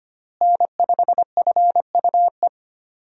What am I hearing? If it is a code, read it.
Morse code, heard as a keyed tone, about 25 wpm, 700 Hz. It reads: N5FUE
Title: USA Call Signs - 5 Characters - Repeated 2x - 25wpm